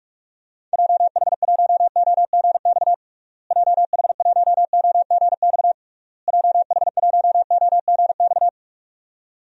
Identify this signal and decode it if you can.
Morse code, heard as a keyed tone, about 46 wpm, 700 Hz. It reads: JH1OGX JH1OGX JH1OGX
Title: DXCC-100 Call Signs - 6 Characters - Repeated 3x - 45wpm